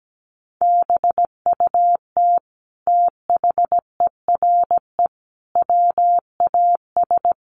Morse code, heard as a keyed tone, about 17 wpm, 700 Hz. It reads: BUT THERE WAS
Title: Sentences from Top 100 Words 17wpm